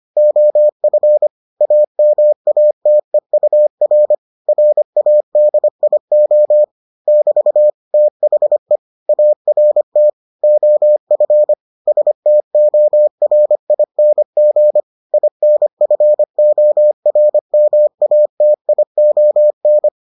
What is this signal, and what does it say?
Morse code, heard as a keyed tone, about 25 wpm, 600 Hz. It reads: OF AMATEUR RADIO <BT> THE ART OF STORING INFORMATION